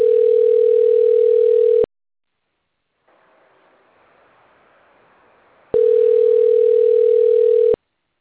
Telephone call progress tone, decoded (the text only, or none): ringback tone